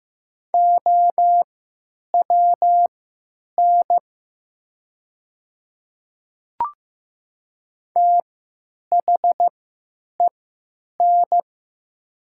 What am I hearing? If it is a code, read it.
Morse code, heard as a keyed tone, about 15 wpm, spaced as for 9 wpm, 700 Hz. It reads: OWN THEN
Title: 3x Character Spacing - Top 300 Words - 15wpm